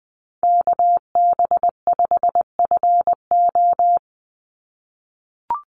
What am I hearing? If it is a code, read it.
Morse code, heard as a keyed tone, about 19 wpm, 700 Hz. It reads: KB5FO
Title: USA Call Signs 20wpm